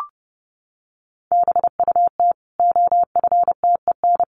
Morse code, heard as a keyed tone, about 30 wpm, 700 Hz. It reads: BUT OFTEN
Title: Sentences from Top 1400 Words 30wpm